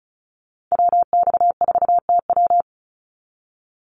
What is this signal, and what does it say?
Morse code, heard as a keyed tone, about 34 wpm, 700 Hz. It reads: WX4TW
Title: NA SOTA Activator Callsigns 35wpm